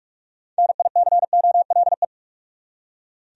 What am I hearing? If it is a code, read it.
Morse code, heard as a keyed tone, about 45 wpm, 700 Hz. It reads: NICOLE
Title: USA Common Names - 45wpm